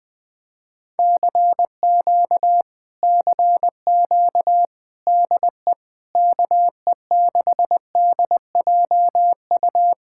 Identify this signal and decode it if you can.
Morse code, heard as a keyed tone, about 20 wpm, 700 Hz. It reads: CQ CQ DE KE6DJU
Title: Speed-Racing Common QSO Phrases - 17wpm